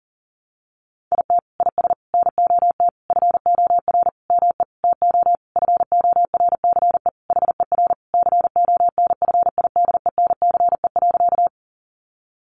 Morse code, heard as a keyed tone, about 40 wpm, 700 Hz. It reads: IT IS NOT FOR ME TO FORCE HER CONFIDENCE.
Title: Sentences from Top 1500 Words 40wpm